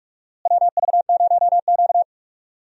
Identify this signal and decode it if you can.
Morse code, heard as a keyed tone, about 46 wpm, 700 Hz. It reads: WV0Q
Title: USA Call Signs - 4 Characters - Repeated 3x - 45wpm